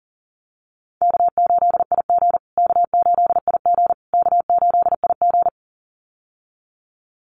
Morse code, heard as a keyed tone, about 40 wpm, 700 Hz. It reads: K8IG K8IG K8IG
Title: USA Call Signs Repeated 3x 40wpm